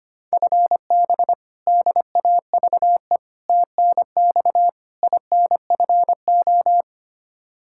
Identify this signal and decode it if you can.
Morse code, heard as a keyed tone, about 25 wpm, 700 Hz. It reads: FB DAVE TNX INFO
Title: K1USN SST QSO Practice 25wpm